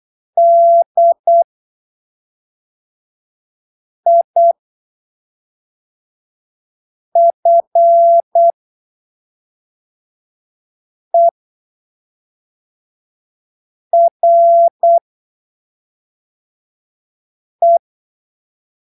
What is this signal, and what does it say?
Morse code, heard as a keyed tone, about 8 wpm, spaced as for 3 wpm, 680 Hz. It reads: DIFERE
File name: arquivos / texto / texto 12 08ppm